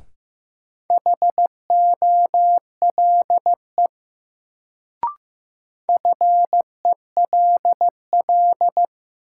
Morse code, heard as a keyed tone, about 15 wpm, 700 Hz. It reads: HOLE FELL
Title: Sentences from Top 1600 Words - New Word Drill - 15wpm